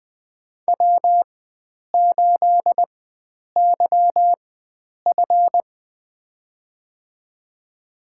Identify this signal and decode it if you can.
Morse code, heard as a keyed tone, about 20 wpm, spaced as for 9 wpm, 700 Hz. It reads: W8YF
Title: USA Call Signs - 4 Characters - 4x Character Spacing - 20wpm